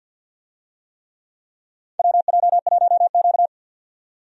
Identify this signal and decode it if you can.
Morse code, heard as a keyed tone, about 53 wpm, 700 Hz. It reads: WJ1Q
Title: USA Call Signs - 4 Characters - 50wpm